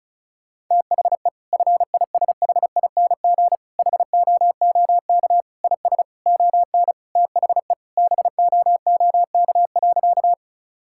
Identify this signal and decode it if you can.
Morse code, heard as a keyed tone, about 36 wpm, 700 Hz. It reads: THE FISHING HOOK IS ON THE BOOK.
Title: Sentences from Top 1600 Words 35wpm